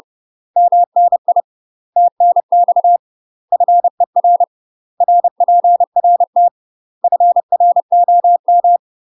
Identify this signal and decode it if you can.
Morse code, heard as a keyed tone, about 30 wpm, 700 Hz. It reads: MNI TNX FER RPRT FROM
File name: UK MASTER QSOs TRAINING / 30WPM / UK MASTER QSOs 0028